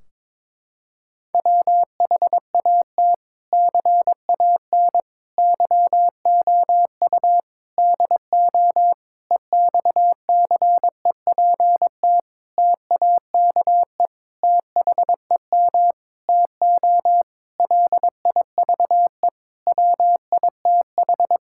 Morse code, heard as a keyed tone, about 22 wpm, 700 Hz. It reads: WHAT CAN YOU DO EXCEPT TAKE THEM TO LIVE WITH
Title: Sentences from Top 1600 Words 22wpm